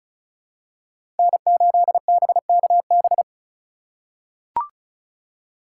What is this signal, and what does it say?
Morse code, heard as a keyed tone, about 34 wpm, 700 Hz. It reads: N8BKB
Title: USA Call Signs - 5 Characters - 4x Character Spacing - 35wpm